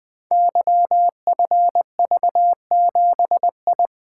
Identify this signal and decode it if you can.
Morse code, heard as a keyed tone, about 20 wpm, 700 Hz. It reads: YFV7I
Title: Code Groups - 5 Letter:Numbers - 20wpm